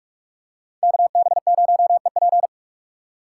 Morse code, heard as a keyed tone, about 45 wpm, 700 Hz. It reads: KB0EP